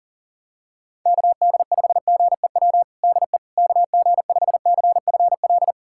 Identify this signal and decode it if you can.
Morse code, heard as a keyed tone, about 40 wpm, 690 Hz. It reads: KD5GEW DE KG5CFL
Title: USA Callsign DE Callsign - 6 Characters - 40wpm